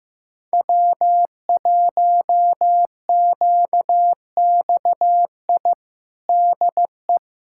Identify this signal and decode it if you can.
Morse code, heard as a keyed tone, about 16 wpm, 700 Hz. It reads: W1QXI DE
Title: USA Callsign DE Callsign - 5 Characters - 15wpm